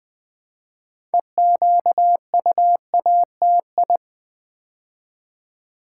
Morse code, heard as a keyed tone, about 20 wpm, 700 Hz. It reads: EQUATI
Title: Build A Word - 3 Letters At A Time - 8-Letter Words - 20wpm